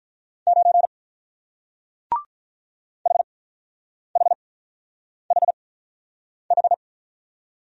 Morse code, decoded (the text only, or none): 9 HHHH